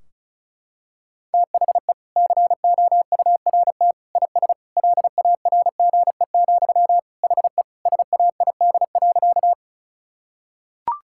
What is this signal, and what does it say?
Morse code, heard as a keyed tone, about 36 wpm, 700 Hz. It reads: THE COURT IS LARGE, HE SAID.
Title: Sentences from Top 1600 Words 35wpm